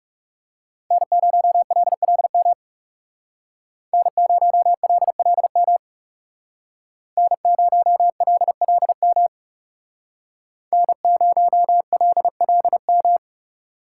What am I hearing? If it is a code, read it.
Morse code, heard as a keyed tone, about 39 wpm, 700 Hz. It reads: N0LLM N0LLM N0LLM N0LLM